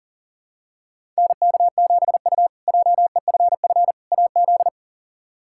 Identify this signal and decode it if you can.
Morse code, heard as a keyed tone, about 40 wpm, 700 Hz. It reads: NK7U JEFF AZ
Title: K1USN SST QSO Practice - 40wpm